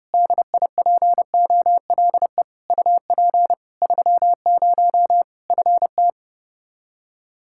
Morse code, heard as a keyed tone, about 32 wpm, 700 Hz. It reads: DIPOLE UP 30 FT
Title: Practice QSO - Exchange 2 - Info - 30wpm